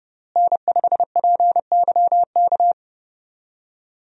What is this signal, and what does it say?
Morse code, heard as a keyed tone, about 30 wpm, 700 Hz. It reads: N5PYK
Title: USA Call Signs - 5 Characters - Repeated 2x - 30wpm